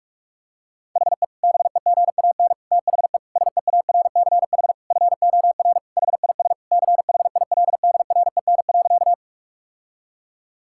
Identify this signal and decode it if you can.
Morse code, heard as a keyed tone, about 41 wpm, 690 Hz. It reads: HE BEGAN THE SEARCH FOR HIS CHILDREN.
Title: Sentences from Top 1000 Words - Review Part 5 - 45wpm